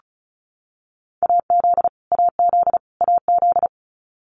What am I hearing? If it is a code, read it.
Morse code, heard as a keyed tone, about 34 wpm, 700 Hz. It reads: AZ AZ AZ
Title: US States Repeated 3x 35wpm